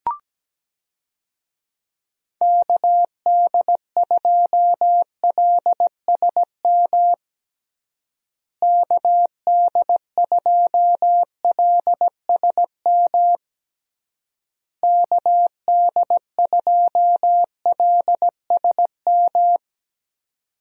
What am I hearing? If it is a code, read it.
Morse code, heard as a keyed tone, about 17 wpm, 700 Hz. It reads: KD2LSM KD2LSM KD2LSM
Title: USA Call Signs - 6 Characters - Repeated 3x - 17wpm